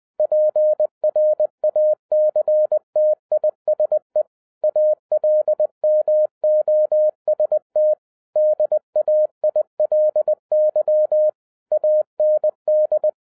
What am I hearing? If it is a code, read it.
Morse code, heard as a keyed tone, about 20 wpm, 610 Hz. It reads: PRACTISE ALMOST DAILY AND
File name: 2011-07-mp3 / n08 A LOT HAS CHANGED 20wpm